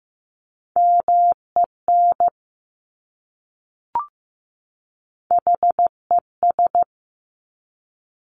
Morse code, heard as a keyed tone, about 15 wpm, 690 Hz. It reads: MEN HES
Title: Top Three Letter Combinations in Words 15wpm